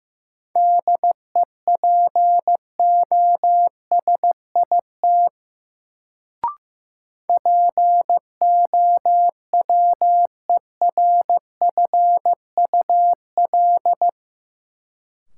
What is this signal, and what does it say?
Morse code, heard as a keyed tone, about 15 wpm, 700 Hz. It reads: DEPOSIT POWERFUL
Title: Sentences from Top 1300 Words - New Word Drill Alt - 15wpm